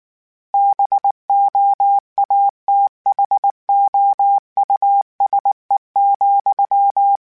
Morse code, heard as a keyed tone, about 19 wpm, 800 Hz. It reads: BOATHOUSE,
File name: General / General-mp3s / Houses @18